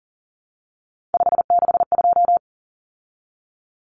Morse code, heard as a keyed tone, about 40 wpm, 700 Hz. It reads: 562
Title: Numbers - 3-Digit 40wpm